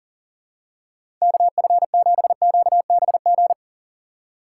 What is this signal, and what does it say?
Morse code, heard as a keyed tone, about 40 wpm, 700 Hz. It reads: KF7QBG
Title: USA Call Signs - 6 Characters - 3x Character Spacing - 40wpm